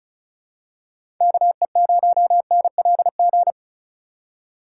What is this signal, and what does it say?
Morse code, heard as a keyed tone, about 36 wpm, 700 Hz. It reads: KE0NLG
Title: USA Call Signs - 4x Character Spacing - 35wpm